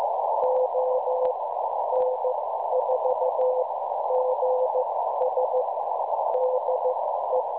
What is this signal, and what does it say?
Morse code, heard as a keyed tone, about 14 wpm, spaced as for 11 wpm, 510 Hz, with noise. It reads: ON4GS DE
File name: cwronde / websdr recording .6kHz